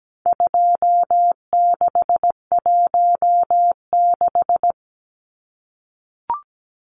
Morse code, heard as a keyed tone, about 17 wpm, 700 Hz. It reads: 2616
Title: Numbers - 4-Digit 17wpm